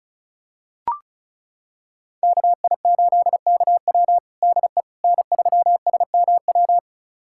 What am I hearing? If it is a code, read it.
Morse code, heard as a keyed tone, about 36 wpm, 700 Hz. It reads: KI8KW DE N3SMW
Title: USA Callsign DE Callsign - 5 Characters - 35wpm